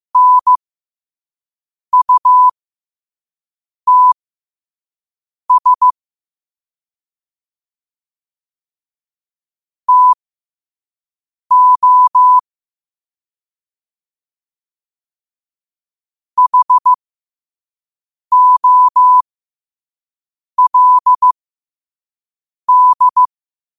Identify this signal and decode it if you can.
Morse code, heard as a keyed tone, about 15 wpm, spaced as for 5 wpm, 1000 Hz. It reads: NUTS TO HOLD
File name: qotdfiles / 5wpm